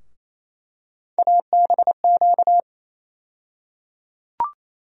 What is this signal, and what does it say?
Morse code, heard as a keyed tone, about 27 wpm, 700 Hz. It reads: ABQ
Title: USA Call Sign Postfixes - 3 Letters 28wpm